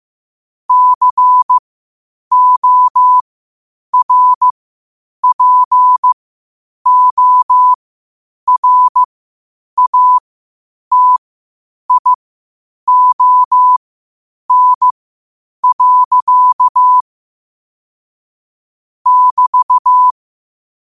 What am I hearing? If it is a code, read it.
Morse code, heard as a keyed tone, about 15 wpm, spaced as for 8 wpm, 1000 Hz. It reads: CORPORATION. <BT>
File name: qotdfiles / 8wpm